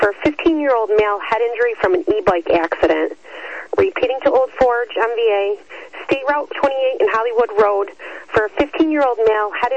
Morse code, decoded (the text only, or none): none